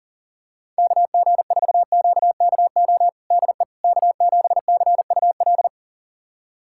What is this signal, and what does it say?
Morse code, heard as a keyed tone, about 40 wpm, 700 Hz. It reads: KG4QKO DE K7CUL
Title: USA Callsign DE Callsign 40wpm